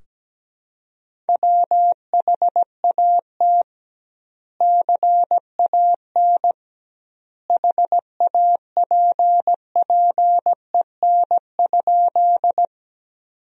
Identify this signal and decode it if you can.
Morse code, heard as a keyed tone, about 17 wpm, 700 Hz. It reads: WHAT CAN HAPPEN?
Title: 2x Word-Spacing - Sentences from Top 400 Words - 17wpm